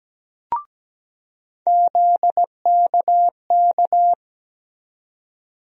Morse code, heard as a keyed tone, about 17 wpm, 700 Hz. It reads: ZKK
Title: USA Call Sign Postfixes - 3 Letters 17wpm